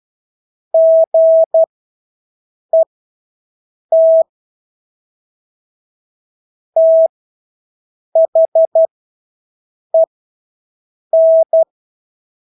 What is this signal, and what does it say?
Morse code, heard as a keyed tone, about 12 wpm, spaced as for 6 wpm, 650 Hz. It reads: GET THEN
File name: Engelske ord 12-6 wpm